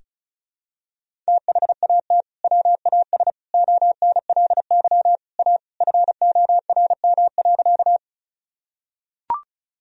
Morse code, heard as a keyed tone, about 34 wpm, 700 Hz. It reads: THAT WAS ONLY A FORM.
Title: Sentences from Top 1000 Words - Review Part 1 - 35wpm